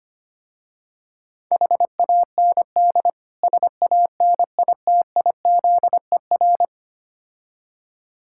Morse code, heard as a keyed tone, about 25 wpm, 700 Hz. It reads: HAND SANITIZER
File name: practice / 25wpm / Difficult Words with Hints - 25wpm